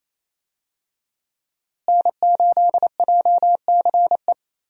Morse code, heard as a keyed tone, about 28 wpm, 700 Hz. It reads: N8JCE
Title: USA Call Signs - 5 Characters - Repeated 3x - 28wpm